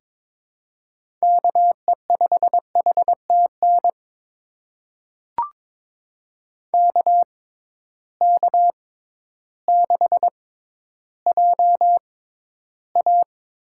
Morse code, heard as a keyed tone, about 22 wpm, spaced as for 12 wpm, 700 Hz. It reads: KE5HTN KK6JA
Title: USA Call Signs - 6x Character Spacing - 22wpm